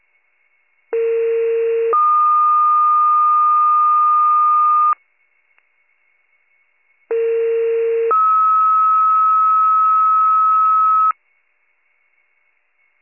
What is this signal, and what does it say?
Two-tone page, heard as a one-second tone, then a three-second tone, about 460 then 1220 Hz, about 460 then 1290 Hz. It is heard from 0.9 to 11.1 s.